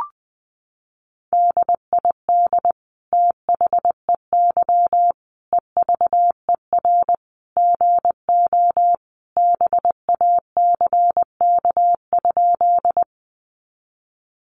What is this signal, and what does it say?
Morse code, heard as a keyed tone, about 20 wpm, 700 Hz. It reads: DID THEY EVER GO BACK?
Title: Sentences from Top 400 Words 20wpm